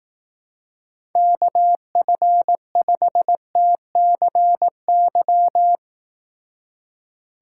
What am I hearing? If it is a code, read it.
Morse code, heard as a keyed tone, about 18 wpm, 700 Hz. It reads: KF5TCY
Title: Speed Racing - USA 2x3 Call Signs 15wpm